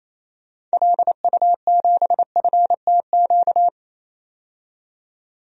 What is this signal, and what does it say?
Morse code, heard as a keyed tone, about 28 wpm, 700 Hz. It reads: LU7FTQ
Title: DXCC-100 Call Signs - 6 Characters - 28wpm